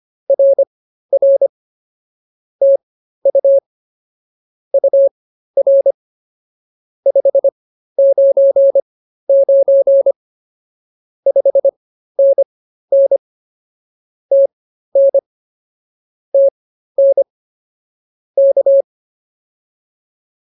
Morse code, heard as a keyed tone, about 24 wpm, spaced as for 13 wpm, 550 Hz. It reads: RR TU UR 599 5NN TN TN K
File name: POTA207 13